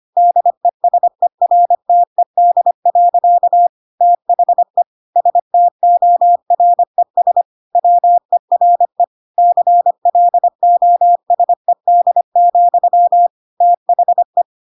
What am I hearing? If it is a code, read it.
Morse code, heard as a keyed tone, about 25 wpm, 700 Hz. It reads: DESERTED. THE STORES WERE CLOSED, THE